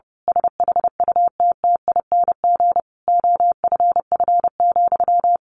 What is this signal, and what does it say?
Morse code, heard as a keyed tone, about 30 wpm, 700 Hz. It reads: SHUTTING OFF,
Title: A Sound Of Thunder 30wpm